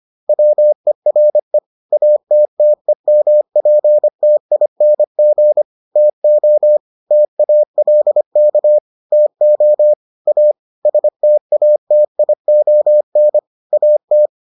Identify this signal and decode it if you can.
Morse code, heard as a keyed tone, about 25 wpm, 600 Hz. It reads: WERE ATTEMPTING TO TALK TO A STATION AT